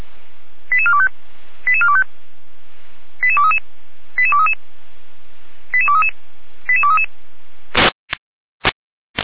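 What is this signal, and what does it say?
Five-tone selective call (ZVEI), heard as quick runs of five tones, tones of about 70 ms. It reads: 88425 88425 88233 88233 88233 88233
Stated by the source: ZVEI five-tone selective call received by SDR.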